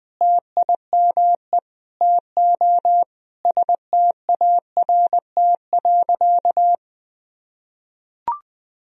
Morse code, heard as a keyed tone, about 20 wpm, 700 Hz. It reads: TIME TO START.
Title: Sentences from Top 1000 Words - Review Part 1 - 20wpm